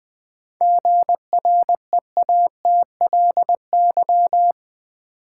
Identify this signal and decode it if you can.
Morse code, heard as a keyed tone, about 20 wpm, 700 Hz. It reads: GREATLY
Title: Sentences from Top 1400 Words - New Word Drill - 20wpm